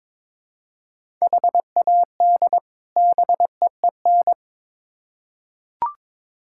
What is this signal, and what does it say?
Morse code, heard as a keyed tone, about 22 wpm, 700 Hz. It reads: HAD BEEN
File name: Build A Phrase - 2-Word Phrases - 22wpm